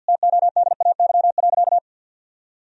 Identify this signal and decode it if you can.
Morse code, heard as a keyed tone, about 50 wpm, 700 Hz. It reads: TODAY.
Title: Sentences from Top 1000 Words - Review Part 4 - 50wpm